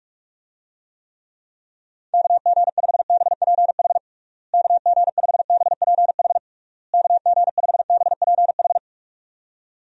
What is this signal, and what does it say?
Morse code, heard as a keyed tone, about 43 wpm, 700 Hz. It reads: KG5BPH KG5BPH KG5BPH
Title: USA Call Signs - 6 Characters - Repeated 3x - 45wpm